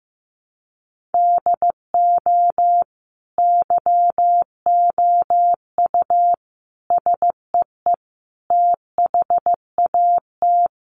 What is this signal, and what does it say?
Morse code, heard as a keyed tone, about 14 wpm, 700 Hz. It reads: DO YOU SEE THAT
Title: Sentences from Top 500 Words 15wpm